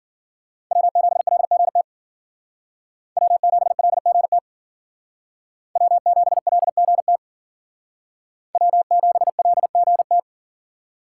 Morse code, heard as a keyed tone, about 51 wpm, 700 Hz. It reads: W7LGT W7LGT W7LGT W7LGT